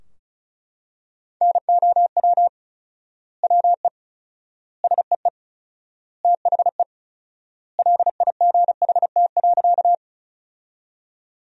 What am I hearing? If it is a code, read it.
Morse code, heard as a keyed tone, about 36 wpm, 700 Hz. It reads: NOW WE SEE THE LIGHT.
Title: 4x Word-Spacing - Sentences from Top 200 Words - 35wpm